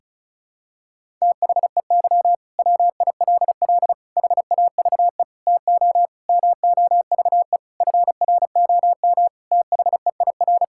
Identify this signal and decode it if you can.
Morse code, heard as a keyed tone, about 34 wpm, 700 Hz. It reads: THEY WILL HAVE TO MOVE FROM THEIR
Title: Sentences from Top 1000 Words - Review Part 6 - 35wpm